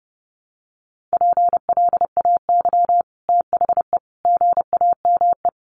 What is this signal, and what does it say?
Morse code, heard as a keyed tone, about 30 wpm, 700 Hz. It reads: PLAY THE GAME
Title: Sentences from Top 500 Words 30wpm